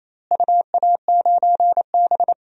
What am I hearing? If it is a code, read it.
Morse code, heard as a keyed tone, about 28 wpm, 700 Hz. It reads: UA9B
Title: DXCC-100 Call Signs - 4 Characters - Repeated 2x - 28wpm